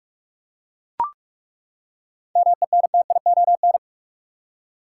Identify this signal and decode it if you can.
Morse code, heard as a keyed tone, about 42 wpm, 700 Hz. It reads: MENTION
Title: Seven Letter Words - 45wpm